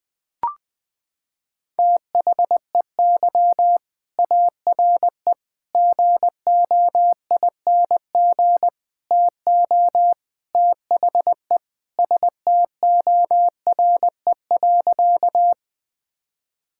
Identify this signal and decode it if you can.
Morse code, heard as a keyed tone, about 21 wpm, 700 Hz. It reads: THEY ARE GOING TO THE STORE.
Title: Sentences with 4-Word Phrases - 20wpm